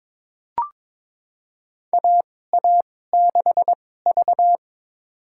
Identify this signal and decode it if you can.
Morse code, heard as a keyed tone, about 22 wpm, spaced as for 16 wpm, 700 Hz. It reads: AA6V
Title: USA Call Signs - 4 Characters - 2x Character Spacing - 22wpm